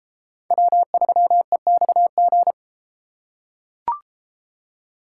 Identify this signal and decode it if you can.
Morse code, heard as a keyed tone, about 32 wpm, 700 Hz. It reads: W3EXG